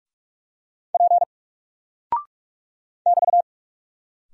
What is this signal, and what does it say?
Morse code, heard as a keyed tone, about 44 wpm, 700 Hz. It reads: P <BT>
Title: Single Character (Letter:Number:Punctuation:Prosign) Alt 45wpm